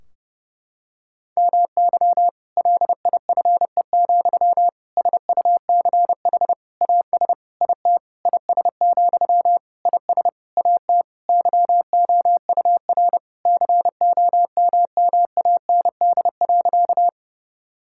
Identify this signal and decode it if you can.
Morse code, heard as a keyed tone, about 30 wpm, 700 Hz. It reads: MY LIFE, SUCH AS IT IS, IS AT YOUR COMMAND.
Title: Sentences from Top 1600 Words 30wpm